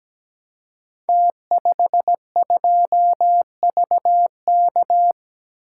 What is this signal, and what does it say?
Morse code, heard as a keyed tone, about 17 wpm, 700 Hz. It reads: T52VK
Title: Code Groups - 5 Letter:Numbers - 17wpm